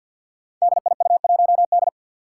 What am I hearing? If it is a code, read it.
Morse code, heard as a keyed tone, about 53 wpm, 700 Hz. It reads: DIU1D
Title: Five Challenging Letter-Number Combinations 50wpm